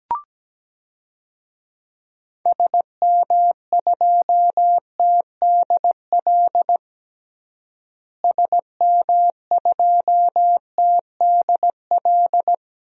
Morse code, heard as a keyed tone, about 17 wpm, 700 Hz. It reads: SM2TDL SM2TDL
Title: DXCC-100 Call Signs - 6 Characters - Repeated 3x - 17wpm